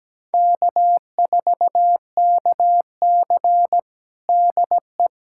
Morse code, heard as a keyed tone, about 17 wpm, 700 Hz. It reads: K4KC DE
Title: USA Callsign DE Callsign - 4 Characters - 17wpm